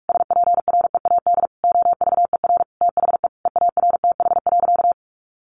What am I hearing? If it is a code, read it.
Morse code, heard as a keyed tone, about 44 wpm, 700 Hz. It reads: SPREAD OVER THE EARTH.
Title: Sentences from Top 1200 Words 45wpm